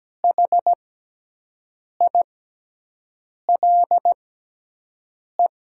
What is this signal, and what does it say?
Morse code, heard as a keyed tone, about 17 wpm, spaced as for 6 wpm, 700 Hz. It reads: HILE
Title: 6x Character Spacing - Top 500 Words - 17wpm